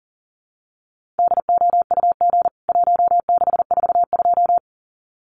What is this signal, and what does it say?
Morse code, heard as a keyed tone, about 40 wpm, 700 Hz. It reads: DOUG 1642
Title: CWT Exchange 40wpm